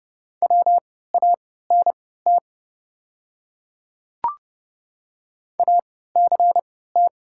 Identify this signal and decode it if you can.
Morse code, heard as a keyed tone, about 31 wpm, spaced as for 17 wpm, 690 Hz. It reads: WANT ACT
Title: 3x Character Spacing - Top 200 Words - 30wpm